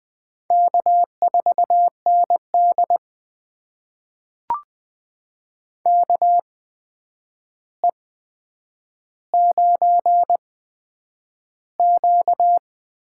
Morse code, heard as a keyed tone, about 20 wpm, spaced as for 9 wpm, 700 Hz. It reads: K4ND KE9Q
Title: USA Call Signs - 4 Characters - 8x Character Spacing - 20wpm